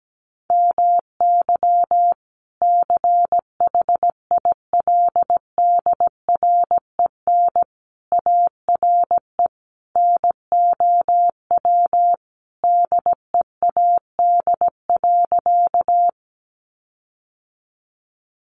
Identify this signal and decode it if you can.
Morse code, heard as a keyed tone, about 17 wpm, 700 Hz. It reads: MY CHILDREN ARE NOW DEAD.
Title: Sentences from Top 1500 Words 17wpm